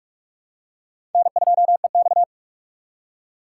Morse code, heard as a keyed tone, about 46 wpm, 700 Hz. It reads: N2EX
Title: USA Call Signs - 4 Characters - 4x Character Spacing - 45wpm